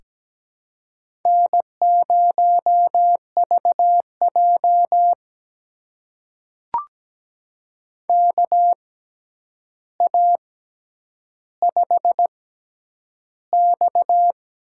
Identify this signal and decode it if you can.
Morse code, heard as a keyed tone, about 17 wpm, spaced as for 9 wpm, 700 Hz. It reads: N0VJ KA5X
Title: USA Call Signs - 4 Characters - 6x Character Spacing - 17wpm